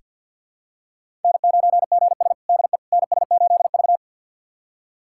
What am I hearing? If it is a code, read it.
Morse code, heard as a keyed tone, about 48 wpm, 700 Hz. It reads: N9GS DE NS8V
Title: USA Callsign DE Callsign - 4 Characters - 50wpm